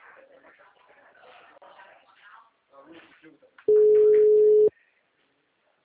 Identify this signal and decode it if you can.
Ringback tone (Europe, 425 Hz), with speech.